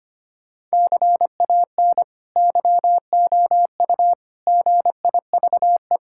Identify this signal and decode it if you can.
Morse code, heard as a keyed tone, about 25 wpm, 700 Hz. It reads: CAN YOU GIVE
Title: Sentences from Top 1000 Words - Review Part 6 - 25wpm